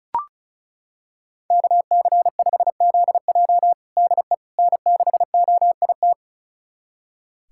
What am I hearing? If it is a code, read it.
Morse code, heard as a keyed tone, about 36 wpm, 700 Hz. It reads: KC5ZJ DE N6OIT
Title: USA Callsign DE Callsign - 5 Characters - 35wpm